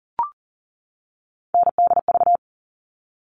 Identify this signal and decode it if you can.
Morse code, heard as a keyed tone, about 40 wpm, 700 Hz. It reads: NDV
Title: All Cut Numbers - 3-Digit 40wpm